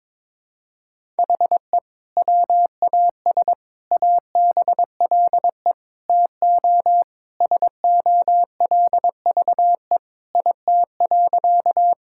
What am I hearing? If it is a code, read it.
Morse code, heard as a keyed tone, about 22 wpm, 700 Hz. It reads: HE WAS ABLE TO SOLVE IT.
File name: practice / 22wpm / Sentences with 3-Word Phrases - 22wpm